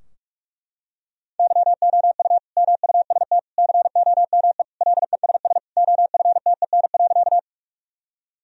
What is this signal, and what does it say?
Morse code, heard as a keyed tone, about 41 wpm, 700 Hz. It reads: YOU MUST COME LESS OFTEN.
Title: Sentences from Top 1000 Words - Review Part 2 - 45wpm